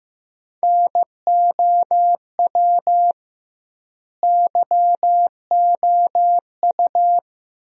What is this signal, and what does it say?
Morse code, heard as a keyed tone, about 15 wpm, 700 Hz. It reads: NOW YOU
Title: 2x Word-Spacing - Sentences from Top 100 Words - 15wpm